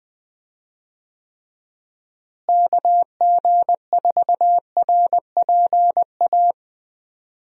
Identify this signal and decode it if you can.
Morse code, heard as a keyed tone, about 20 wpm, 700 Hz. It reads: KG4RPA
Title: USA Call Signs - 6 Characters - Repeated 3x - 20wpm